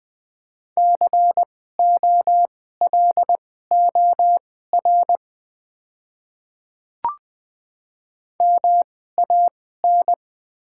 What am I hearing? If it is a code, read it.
Morse code, heard as a keyed tone, about 20 wpm, spaced as for 14 wpm, 700 Hz. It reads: COLOR MAN
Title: 2x Character Spacing - Top 300 Words - 20wpm